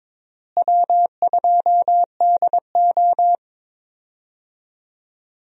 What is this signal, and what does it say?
Morse code, heard as a keyed tone, about 22 wpm, 700 Hz. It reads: W2DO
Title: USA Call Signs - 4 Characters - Repeated 3x - 22wpm